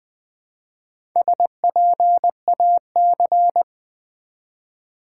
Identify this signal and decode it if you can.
Morse code, heard as a keyed tone, about 20 wpm, 700 Hz. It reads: SPAC
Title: Build A Word - 1 Letter At A Time - 5-Letter Words - 20wpm